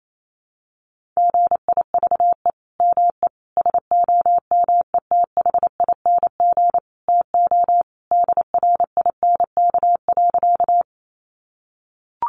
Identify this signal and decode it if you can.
Morse code, heard as a keyed tone, about 28 wpm, 700 Hz. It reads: GIVE ME SOMETHING TO DRINK.
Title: Sentences from Top 1500 Words 28wpm